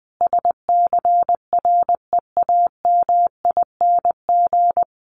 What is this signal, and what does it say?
Morse code, heard as a keyed tone, about 20 wpm, 700 Hz. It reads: SCREAMING
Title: A Sound Of Thunder 20wpm